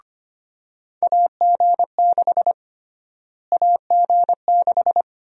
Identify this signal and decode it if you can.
Morse code, heard as a keyed tone, about 25 wpm, 700 Hz. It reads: AG6 AG6
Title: USA Call Sign Prefixes - Repeated 2x - 25wpm